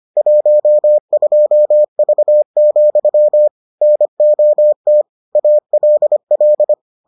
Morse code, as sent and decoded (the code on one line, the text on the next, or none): .---- ..--- ...- --..-- / -. --- - / .- .-.. .-..
12V, NOT ALL